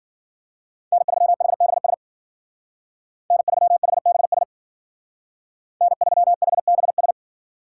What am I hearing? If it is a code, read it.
Morse code, heard as a keyed tone, about 49 wpm, 700 Hz. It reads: N3HBS N3HBS N3HBS
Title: Speed Racing - USA 1x3 Call Signs 40wpm